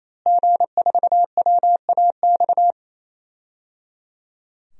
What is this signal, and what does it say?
Morse code, heard as a keyed tone, about 28 wpm, 700 Hz. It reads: G4WAX
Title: DXCC-100 Call Signs - 5 Characters - Repeated 3x - 28wpm